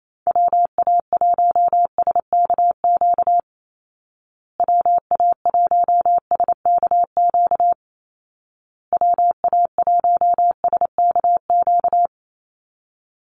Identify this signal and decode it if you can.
Morse code, heard as a keyed tone, about 28 wpm, 700 Hz. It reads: WA1SKQ WA1SKQ WA1SKQ
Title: Call Signs Repeated 2x - 28wpm